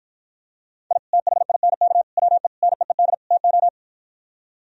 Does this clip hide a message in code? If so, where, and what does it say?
Yes: Morse code from 0.9 to 3.7 s, heard as a keyed tone, about 52 wpm, 700 Hz. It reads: I THINK WE NEED TO